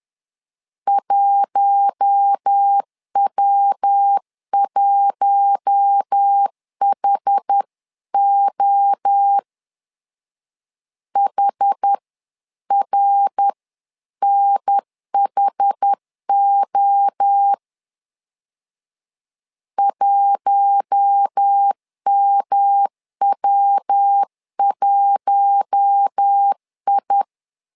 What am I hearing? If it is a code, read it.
Morse code, heard as a keyed tone, about 10 wpm, 800 Hz. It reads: 1W1HO HRNHO 1MW1I
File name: LEZIONI / 4 Q1 10wpm 2448